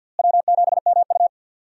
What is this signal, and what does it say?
Morse code, heard as a keyed tone, about 51 wpm, 700 Hz. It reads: W7MU
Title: USA Call Signs - 4 Characters - Repeated 2x - 50wpm